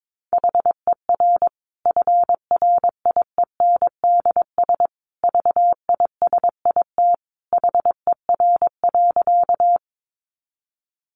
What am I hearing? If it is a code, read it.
Morse code, heard as a keyed tone, about 22 wpm, 700 Hz. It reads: HER FRIENDS VISIT HER.
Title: Easy Phrases 22wpm